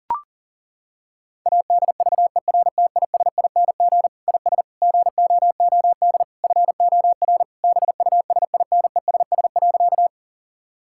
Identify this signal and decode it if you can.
Morse code, heard as a keyed tone, about 40 wpm, 700 Hz. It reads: ADVERTISING IS GOOD FOR BUSINESS.
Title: Sentences from Top 1000 Words - Review Part 6 - 40wpm